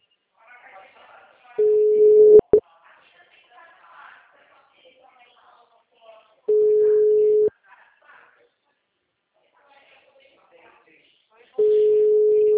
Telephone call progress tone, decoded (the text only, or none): ringback tone